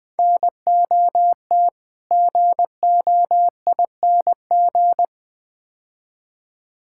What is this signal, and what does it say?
Morse code, heard as a keyed tone, about 20 wpm, 700 Hz. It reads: NOT GOING
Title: Build A Phrase - 4-Word Phrases - 20wpm